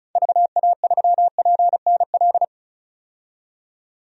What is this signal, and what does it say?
Morse code, heard as a keyed tone, about 34 wpm, 700 Hz. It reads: VA3PNL
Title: DXCC-100 Call Signs - 6 Characters - 35wpm